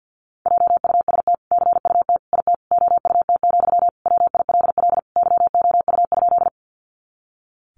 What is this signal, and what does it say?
Morse code, heard as a keyed tone, about 50 wpm, 700 Hz. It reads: JUST CUT IT OUT, WILL YOU?
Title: Sentences from Top 300 Words 50wpm